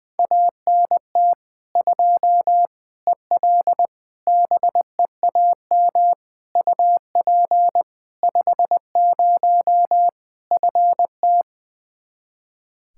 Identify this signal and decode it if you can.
Morse code, heard as a keyed tone, about 20 wpm, 700 Hz. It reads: ANT 2 EL BEAM UP 50 FT
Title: K1USN SST QSO Practice 20wpm